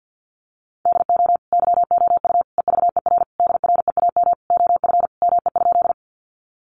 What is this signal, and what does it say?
Morse code, heard as a keyed tone, about 50 wpm, 700 Hz. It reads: DO YOU EVER DREAM OF ME?
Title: Sentences from Top 1500 Words 50wpm